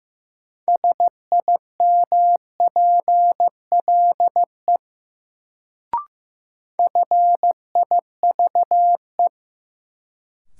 Morse code, heard as a keyed tone, about 15 wpm, 700 Hz. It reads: SIMPLE FIVE
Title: Sentences from Top 400 Words - New Word Drill - 15wpm